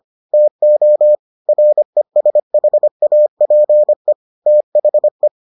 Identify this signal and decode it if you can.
Morse code, heard as a keyed tone, about 25 wpm, 600 Hz. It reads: TO RESHAPE THE